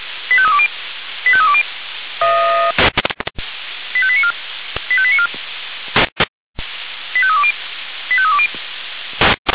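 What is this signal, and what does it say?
Five-tone selective call (ZVEI), heard as quick runs of five tones, tones of about 70 ms. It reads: 85320 85320 85894 85894 85320 85320
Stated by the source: ZVEI five-tone selective call received on an SDR.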